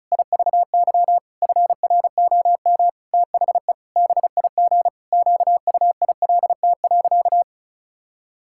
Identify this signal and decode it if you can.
Morse code, heard as a keyed tone, about 34 wpm, 700 Hz. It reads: IVY FROM THE BIG QUILT.
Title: Pangram Challenge v2 - 35wpm